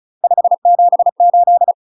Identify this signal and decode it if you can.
Morse code, heard as a keyed tone, about 36 wpm, 700 Hz. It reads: 578
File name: UK MASTER QSOs TRAINING / 35WPM / UK MASTER QSOs 0017